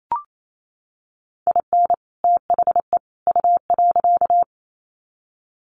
Morse code, heard as a keyed tone, about 28 wpm, 700 Hz. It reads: IN THE U.
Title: Sets of 3 Words 28wpm